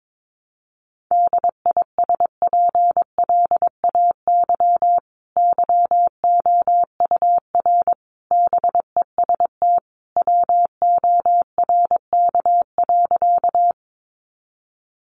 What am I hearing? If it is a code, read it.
Morse code, heard as a keyed tone, about 22 wpm, 700 Hz. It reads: DISPLAY YOUR BEST WORK.
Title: Sentences from Top 1200 Words 22wpm